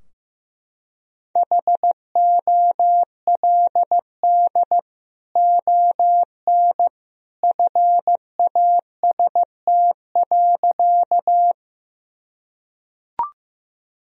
Morse code, heard as a keyed tone, about 15 wpm, 700 Hz. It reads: HOLD ON FAST.
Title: Sentences from Top 1000 Words - Review Part 2 - 15wpm